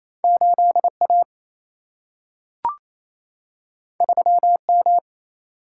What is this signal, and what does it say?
Morse code, heard as a keyed tone, about 28 wpm, 700 Hz. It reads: 8 A 3 M
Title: Single-Letter Single-Number 28wpm